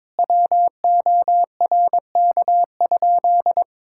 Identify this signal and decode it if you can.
Morse code, heard as a keyed tone, about 22 wpm, 700 Hz. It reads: WORK?
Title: Sentences from Top 1000 Words - Review Part 6 - 22wpm